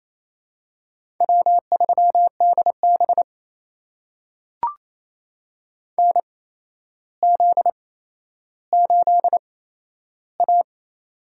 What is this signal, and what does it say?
Morse code, heard as a keyed tone, about 28 wpm, spaced as for 12 wpm, 700 Hz. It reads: W3DB NZ8A
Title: USA Call Signs - 8x Character Spacing - 28wpm